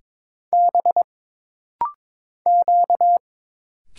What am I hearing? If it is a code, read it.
Morse code, heard as a keyed tone, about 22 wpm, 700 Hz. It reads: B Q
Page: This rapid-fire timing is geared to push your ICR (Instant Character Recognition) skills.